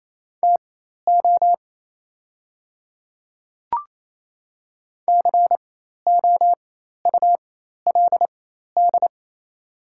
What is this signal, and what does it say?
Morse code, heard as a keyed tone, about 28 wpm, spaced as for 13 wpm, 700 Hz. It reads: TO COULD
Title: 4x Character Spacing - Top 100 Words - 28wpm